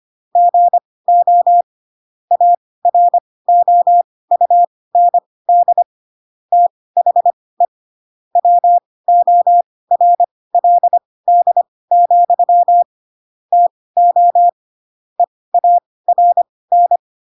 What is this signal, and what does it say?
Morse code, heard as a keyed tone, about 25 wpm, spaced as for 18 wpm, 700 Hz. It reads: GO AROUND THE WORLD, TO EARN
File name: Pinocchio / Pinocchio ch2 12 25 ik1hge